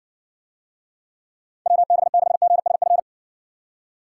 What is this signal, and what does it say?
Morse code, heard as a keyed tone, about 62 wpm, 700 Hz. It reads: WB6GSF